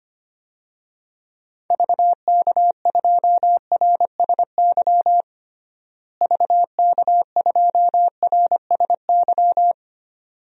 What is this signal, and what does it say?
Morse code, heard as a keyed tone, about 25 wpm, 700 Hz. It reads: VK2RSY VK2RSY
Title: DXCC-100 Call Signs - 6 Characters - Repeated 2x - 25wpm